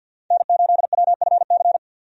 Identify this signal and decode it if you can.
Morse code, heard as a keyed tone, about 45 wpm, 700 Hz. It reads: N8WFC
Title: USA Call Signs - 5 Characters - 50wpm